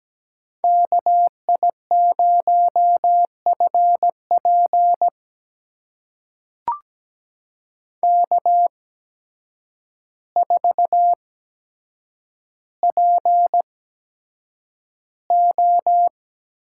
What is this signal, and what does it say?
Morse code, heard as a keyed tone, about 17 wpm, spaced as for 8 wpm, 700 Hz. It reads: KI0FP K4PO
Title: USA Call Signs - 5 Characters - 8x Character Spacing - 17wpm